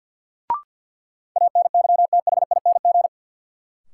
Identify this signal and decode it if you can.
Morse code, heard as a keyed tone, about 50 wpm, 700 Hz. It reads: ANYTHING
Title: Sentences from Top 1000 Words - New Word Drill - 50wpm